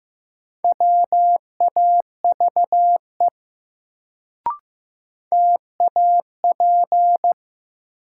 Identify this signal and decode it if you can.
Morse code, heard as a keyed tone, about 16 wpm, 700 Hz. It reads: WAVE TAP
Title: Sentences from Top 1400 Words - New Word Drill - 15wpm